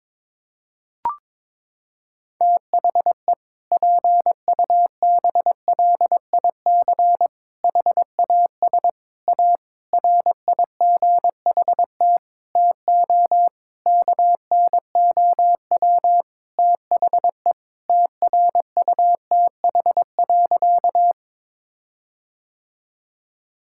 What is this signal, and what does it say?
Morse code, heard as a keyed tone, about 21 wpm, 700 Hz. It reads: THE PUBLIC HAS A RIGHT TO KNOW THE TRUTH.
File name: Sentences with 7-Word Phrases - 22wpm